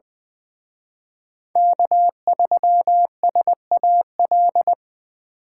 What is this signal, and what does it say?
Morse code, heard as a keyed tone, about 23 wpm, 700 Hz. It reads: K3SAL
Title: Speed Racing - USA 1x3 Call Signs 20wpm